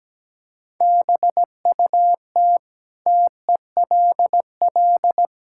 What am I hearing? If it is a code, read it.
Morse code, heard as a keyed tone, about 17 wpm, 700 Hz. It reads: BUT TELL
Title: Sentences from Top 1000 Words - Review Part 6 - 17wpm